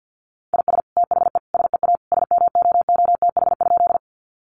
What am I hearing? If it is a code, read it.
Morse code, heard as a keyed tone, about 50 wpm, 700 Hz. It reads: IS THE SEA SMOOTH?
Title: Sentences from Top 1400 Words 50wpm